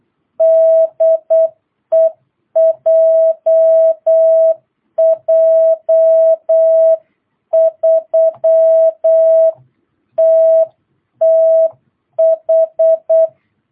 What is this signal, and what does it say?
Morse code, heard as a keyed tone, about 8 wpm, 660 Hz. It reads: DE JJ3TTH
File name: jh8cbh / 2022cw